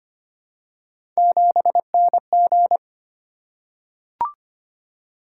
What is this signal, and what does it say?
Morse code, heard as a keyed tone, about 25 wpm, 700 Hz. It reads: 7NG
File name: Three Letter-Number Combinations - 25wpm